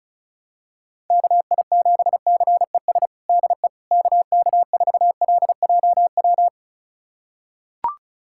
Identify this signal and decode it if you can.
Morse code, heard as a keyed tone, about 36 wpm, 700 Hz. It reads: KI7CES DE KK4LJW
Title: USA Callsign DE Callsign 35wpm